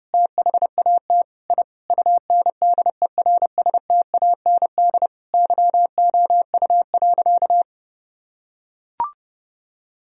Morse code, heard as a keyed tone, about 32 wpm, 700 Hz. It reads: THAT I UNDERSTAND YOU.
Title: Sentences from Top 1000 Words - Review Part 2 - 30wpm